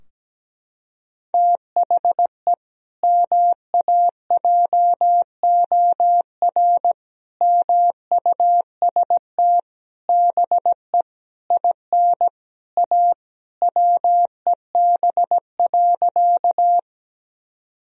Morse code, heard as a keyed tone, about 17 wpm, 700 Hz. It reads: THE MAJOR MUST BE IN A WEB.
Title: Sentences from Top 1000 Words - Review Part 4 - 17wpm